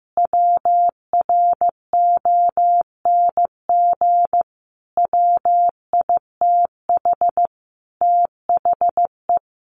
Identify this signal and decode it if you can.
Morse code, heard as a keyed tone, about 15 wpm, 700 Hz. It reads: WRONG WITH THE
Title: Sentences from Top 1500 Words 15wpm